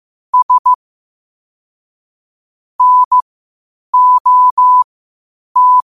Morse code, heard as a keyed tone, about 15 wpm, spaced as for 8 wpm, 1000 Hz. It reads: S NOT
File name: qotdfiles / 8wpm